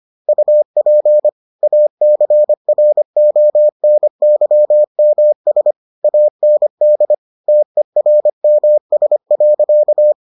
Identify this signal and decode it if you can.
Morse code, heard as a keyed tone, about 25 wpm, 600 Hz. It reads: UP ACRONYMS AND TERMS.